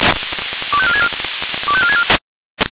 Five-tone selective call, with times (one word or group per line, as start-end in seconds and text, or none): 0.7-1.1 25464
1.7-2.0 25464